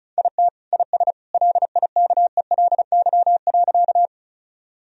Morse code, heard as a keyed tone, about 34 wpm, 700 Hz. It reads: IT IS LIKELY.
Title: Sentences from Top 1000 Words - Review Part 4 - 35wpm